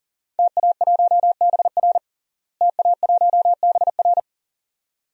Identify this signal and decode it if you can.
Morse code, heard as a keyed tone, about 40 wpm, 700 Hz. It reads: TA1BR TA1BR
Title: DXCC-100 Call Signs - 5 Characters - Repeated 2x - 40wpm